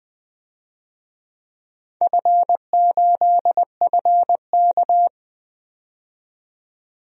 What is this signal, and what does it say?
Morse code, heard as a keyed tone, about 20 wpm, 700 Hz. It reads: F8FK
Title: DXCC-100 Call Signs - 4 Characters - 20wpm